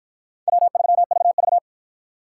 Morse code, heard as a keyed tone, about 48 wpm, 700 Hz. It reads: W3VV